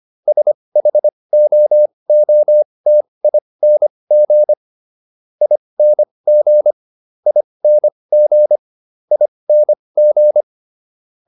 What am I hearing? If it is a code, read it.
Morse code, heard as a keyed tone, about 25 wpm, spaced as for 20 wpm, 600 Hz. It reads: SHOOTING ING ING ING